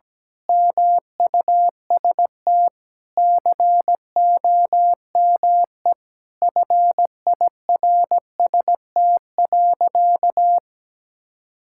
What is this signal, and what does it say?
Morse code, heard as a keyed tone, about 17 wpm, 700 Hz. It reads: MUST COME FIRST.
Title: Sentences from Top 1000 Words - Review Part 6 - 17wpm